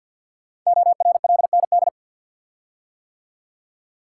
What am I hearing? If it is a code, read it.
Morse code, heard as a keyed tone, about 50 wpm, 700 Hz. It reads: ORLND
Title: Code Groups - 5 Letters - 50wpm